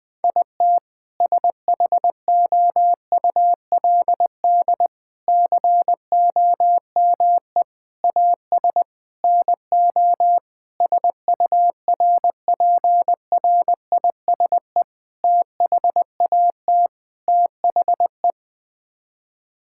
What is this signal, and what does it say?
Morse code, heard as a keyed tone, about 20 wpm, 700 Hz. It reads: IT SHOULD COME AS NO SURPRISE THAT THE
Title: Sets of 8 Words - 20wpm